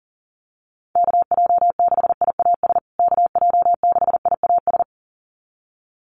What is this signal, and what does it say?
Morse code, heard as a keyed tone, about 40 wpm, 700 Hz. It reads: KJ6IAS KJ6IAS
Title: USA Call Signs Repeated 2x 40wpm